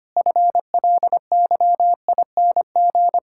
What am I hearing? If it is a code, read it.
Morse code, heard as a keyed tone, about 25 wpm, 700 Hz. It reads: FLYING
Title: Pangram Challenge v2 - 25wpm